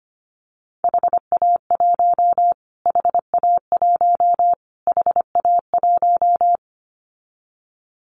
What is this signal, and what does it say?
Morse code, heard as a keyed tone, about 25 wpm, 700 Hz. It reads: HA1 HA1 HA1
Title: Callsign Prefixes for Top 25 DXCC - Repeated 3x - 25wpm